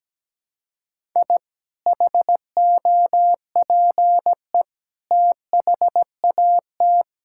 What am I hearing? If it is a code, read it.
Morse code, heard as a keyed tone, about 17 wpm, 700 Hz. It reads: I HOPE THAT
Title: Sentences from Top 1000 Words - Review Part 5 - 17wpm